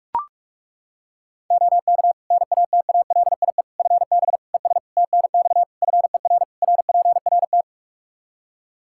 Morse code, heard as a keyed tone, about 41 wpm, 700 Hz. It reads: OK NATALIE FB ES TNX FER RPRT
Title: Practice QSO - Exchange 2 - Info - 45wpm